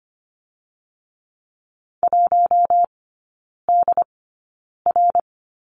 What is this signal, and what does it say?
Morse code, heard as a keyed tone, about 24 wpm, spaced as for 9 wpm, 690 Hz. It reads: 1DR
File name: Lesson 15e - Number 5 - Three Characters 25wpm